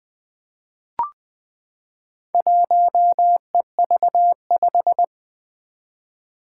Morse code, heard as a keyed tone, about 21 wpm, 700 Hz. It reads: 1EV5
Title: Four Challenging Letter-Number Combinations 20wpm